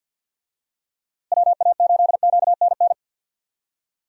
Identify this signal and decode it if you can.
Morse code, heard as a keyed tone, about 50 wpm, 700 Hz. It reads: WA8QNN
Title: USA Call Signs - 6 Characters - Repeated 3x - 50wpm